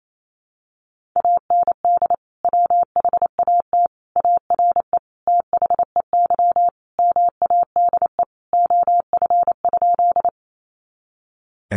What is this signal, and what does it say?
Morse code, heard as a keyed tone, about 28 wpm, 700 Hz. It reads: AND WHAT ARE THEY MADE OF?
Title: Sentences from Top 200 Words 28wpm